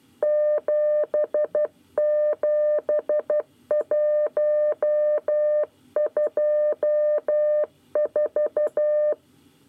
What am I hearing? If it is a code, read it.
Morse code, heard as a keyed tone, about 12 wpm, 600 Hz. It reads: 77124
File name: 9 lezione cw 1